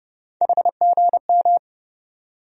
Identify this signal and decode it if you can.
Morse code, heard as a keyed tone, about 30 wpm, 700 Hz. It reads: HGM
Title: USA Call Sign Postfixes - 3 Letters 30wpm